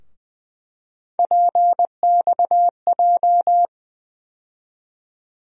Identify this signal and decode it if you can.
Morse code, heard as a keyed tone, about 20 wpm, 700 Hz. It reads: PXJ
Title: USA Call Sign Postfixes - 3 Letters 20wpm